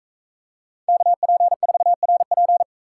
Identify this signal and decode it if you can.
Morse code, heard as a keyed tone, about 38 wpm, 700 Hz. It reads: KP4RP